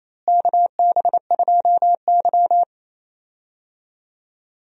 Morse code, heard as a keyed tone, about 28 wpm, 700 Hz. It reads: KB2Y
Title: USA Call Signs - 4 Characters - Repeated 3x - 28wpm